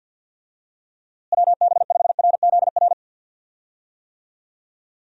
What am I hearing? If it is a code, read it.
Morse code, heard as a keyed tone, about 51 wpm, 700 Hz. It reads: WB5RZR
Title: USA Call Signs - 6 Characters - Repeated 3x - 50wpm